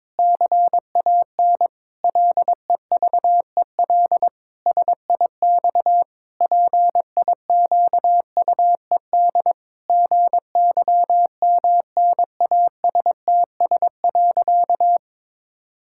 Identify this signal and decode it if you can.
Morse code, heard as a keyed tone, about 22 wpm, 700 Hz. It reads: CAN LEVEL SIX PIQUED GYMNASTS.
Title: Pangram Challenge v2 - 22wpm